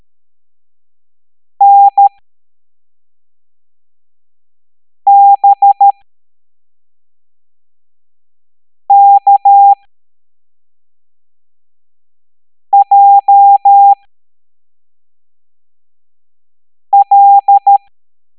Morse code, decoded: NBKJL